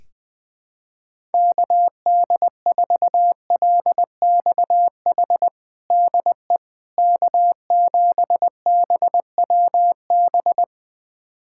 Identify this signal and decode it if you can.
Morse code, heard as a keyed tone, about 21 wpm, 700 Hz. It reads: KD4LXH DE K7BWB
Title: USA Callsign DE Callsign 20wpm